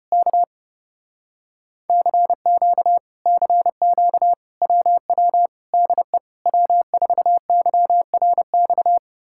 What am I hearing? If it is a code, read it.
Morse code, heard as a keyed tone, about 33 wpm, 700 Hz. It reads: K CQ CQ WW DE W4YRX